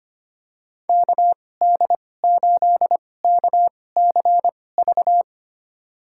Morse code, heard as a keyed tone, about 25 wpm, spaced as for 18 wpm, 700 Hz. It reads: KD8KCV
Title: USA Call Signs - 6 Characters - 2x Character Spacing - 25wpm